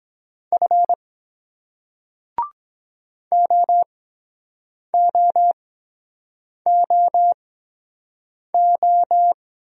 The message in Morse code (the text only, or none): F OOOO